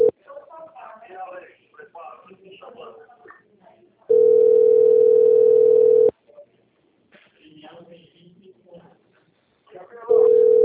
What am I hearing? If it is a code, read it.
Ringback tone (North America), 440 + 480 Hz, with speech.
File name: out- - - - - .10737